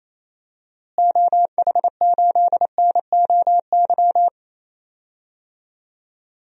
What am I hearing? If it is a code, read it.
Morse code, heard as a keyed tone, about 27 wpm, 700 Hz. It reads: OH8NOY
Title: DXCC-100 Call Signs - 6 Characters - Repeated 3x - 28wpm